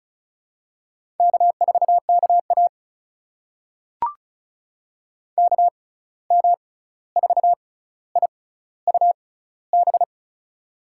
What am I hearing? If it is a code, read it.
Morse code, heard as a keyed tone, about 34 wpm, spaced as for 16 wpm, 700 Hz. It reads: K4KA KM4IUB
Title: USA Call Signs - 6x Character Spacing - 35wpm